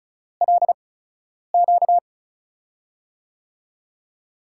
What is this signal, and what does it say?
Morse code, heard as a keyed tone, about 36 wpm, spaced as for 10 wpm, 700 Hz. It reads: LQ
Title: USA Call Signs - 6 Characters - 8x Character Spacing - 35wpm